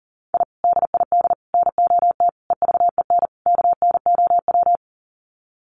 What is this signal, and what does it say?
Morse code, heard as a keyed tone, about 40 wpm, 700 Hz. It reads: I DID NOT EVEN KNOW
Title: Sets of 5 Words Encore 40wpm